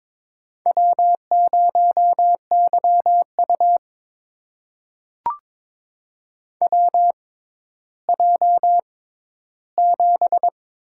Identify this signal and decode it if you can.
Morse code, heard as a keyed tone, about 22 wpm, spaced as for 13 wpm, 700 Hz. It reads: W0YU WJ7
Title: USA Call Signs - 4 Characters - 6x Character Spacing - 22wpm